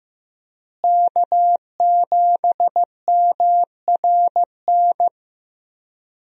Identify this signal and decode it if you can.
Morse code, heard as a keyed tone, about 15 wpm, 700 Hz. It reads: K7MRN
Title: USA Call Signs - 5 Characters - Repeated 3x - 15wpm